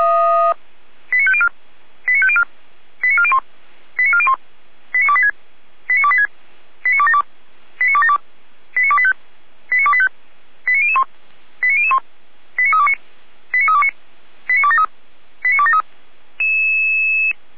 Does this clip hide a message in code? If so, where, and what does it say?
Yes: five-tone selective call (ZVEI) from 1.1 to 15.8 s, heard as quick runs of five tones, tones of about 70 ms. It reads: 89504 89504 89401 89401 89286 89286 89272 89272 89285 89285 89001 89001 89329 89329 89273 89273